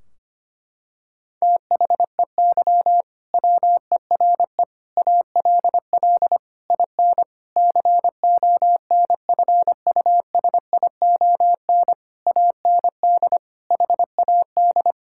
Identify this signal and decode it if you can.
Morse code, heard as a keyed tone, about 25 wpm, 700 Hz. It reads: THEY WERE ALL IN CONFUSION AND HAD
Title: Sentences from Top 1600 Words 25wpm